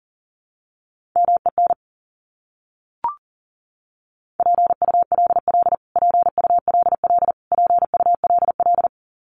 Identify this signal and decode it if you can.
Morse code, heard as a keyed tone, about 37 wpm, 700 Hz. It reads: MEN PULL PULL PULL
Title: Top 500 Words Repeated 40wpm